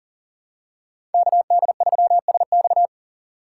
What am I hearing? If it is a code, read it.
Morse code, heard as a keyed tone, about 37 wpm, 700 Hz. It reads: KD3SX
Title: USA Call Signs - 5 Characters - 6x Character Spacing - 40wpm